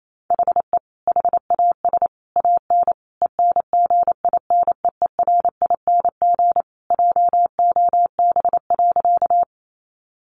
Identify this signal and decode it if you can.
Morse code, heard as a keyed tone, about 28 wpm, 700 Hz. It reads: HE HAS AN ENGINEERING JOB.
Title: Sentences from Top 1500 Words 28wpm